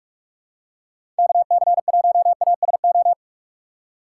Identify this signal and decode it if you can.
Morse code, heard as a keyed tone, about 45 wpm, 700 Hz. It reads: KC1ASO
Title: USA Call Signs - 6 Characters - 3x Character Spacing - 45wpm